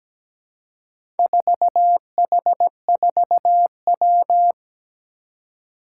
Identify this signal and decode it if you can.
Morse code, heard as a keyed tone, about 17 wpm, 700 Hz. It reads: 4H4W
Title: Four Challenging Letter-Number Combinations 17wpm